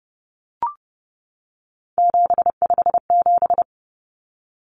Morse code, heard as a keyed tone, about 30 wpm, 700 Hz. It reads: 757
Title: Common Cut Numbers - 3-Digit 30wpm